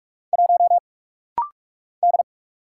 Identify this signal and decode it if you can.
Morse code, heard as a keyed tone, about 44 wpm, 700 Hz. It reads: 1 D
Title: Single Letter-Number - Rapid-Fire 45wpm